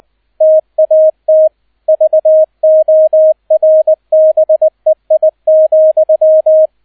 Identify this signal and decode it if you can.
Morse code, heard as a keyed tone, about 20 wpm, 630 Hz, with radio noise. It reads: TAT VORBEI,